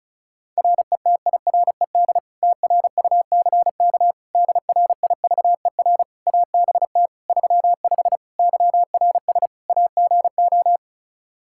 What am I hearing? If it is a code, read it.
Morse code, heard as a keyed tone, about 36 wpm, 700 Hz. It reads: RETIRED TRUCK DRIVER ABT 35 YRS AGO
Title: Practice QSO - Exchange 3 - More Info - 35wpm